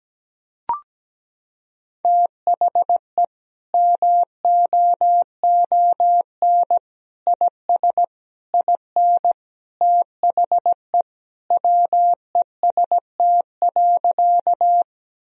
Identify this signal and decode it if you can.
Morse code, heard as a keyed tone, about 17 wpm, 700 Hz. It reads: THE MOON IS IN THE WEST.
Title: Sentences from Top 1000 Words - Review Part 3 - 17wpm